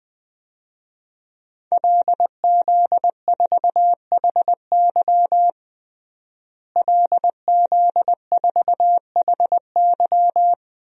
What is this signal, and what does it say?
Morse code, heard as a keyed tone, about 20 wpm, 700 Hz. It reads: LZ4HY LZ4HY
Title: DXCC-100 Call Signs - 5 Characters - Repeated 3x - 20wpm